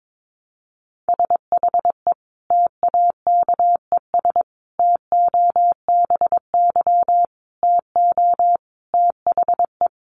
Morse code, heard as a keyed tone, about 22 wpm, 700 Hz. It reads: SHE TAKES TOBY TO THE
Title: Easy Phrases 22wpm